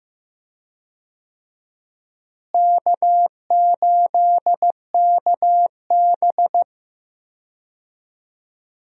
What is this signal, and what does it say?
Morse code, heard as a keyed tone, about 15 wpm, 700 Hz. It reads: K8KB
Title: USA Call Signs - 4 Characters - Repeated 3x - 15wpm